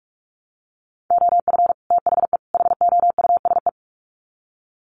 Morse code, heard as a keyed tone, about 44 wpm, 700 Hz. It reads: OF THE HOUSE
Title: Sets of 3 Words 45wpm